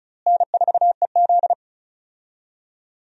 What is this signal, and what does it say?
Morse code, heard as a keyed tone, about 34 wpm, 700 Hz. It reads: N4EZ
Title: USA Call Signs - 4 Characters - Repeated 3x - 35wpm